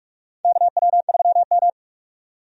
Morse code, heard as a keyed tone, about 47 wpm, 700 Hz. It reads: KW3M
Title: USA Call Signs - 4 Characters - 45wpm